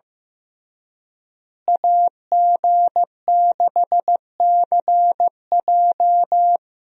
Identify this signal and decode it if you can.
Morse code, heard as a keyed tone, about 16 wpm, 700 Hz. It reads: AG6CJ
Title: USA Call Signs - 5 Characters - Repeated 3x - 15wpm